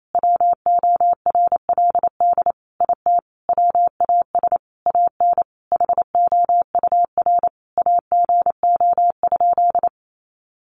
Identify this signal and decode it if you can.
Morse code, heard as a keyed tone, about 28 wpm, 700 Hz. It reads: WORLD IT WAS AN HOUR AGO?
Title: Sentences from Top 1300 Words 28wpm